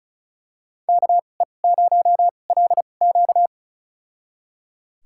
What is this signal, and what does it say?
Morse code, heard as a keyed tone, about 36 wpm, spaced as for 25 wpm, 700 Hz. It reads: KE0LQ
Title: USA Call Signs - 5 Characters - 2x Character Spacing - 35wpm